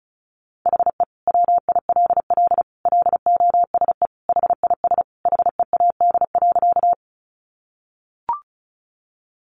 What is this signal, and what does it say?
Morse code, heard as a keyed tone, about 34 wpm, 700 Hz. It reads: HE WILL LOSE HIS HEAD.
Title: Sentences from Top 1100 Words 35wpm